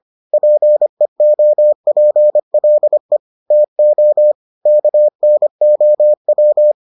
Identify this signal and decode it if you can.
Morse code, heard as a keyed tone, about 25 wpm, 600 Hz. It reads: PEOPLE TO KNOW